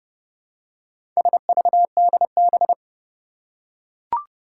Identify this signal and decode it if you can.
Morse code, heard as a keyed tone, about 30 wpm, 700 Hz. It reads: SVDB
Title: Four Challenging Letter-Number Combinations 30wpm